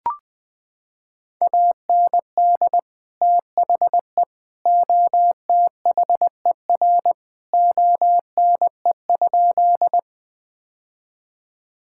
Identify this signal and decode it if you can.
Morse code, heard as a keyed tone, about 20 wpm, 700 Hz. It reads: AND THE OTHER ONE?
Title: Sentences with 3-Word Phrases - 20wpm